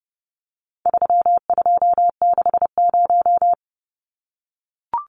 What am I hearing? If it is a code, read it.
Morse code, heard as a keyed tone, about 30 wpm, 700 Hz. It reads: 3260
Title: Numbers - 4-Digit 30wpm